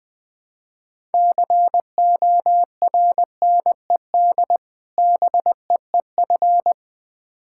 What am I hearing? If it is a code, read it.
Morse code, heard as a keyed tone, about 21 wpm, 700 Hz. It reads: CORNED BEEF
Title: Difficult Words with Hints - 20wpm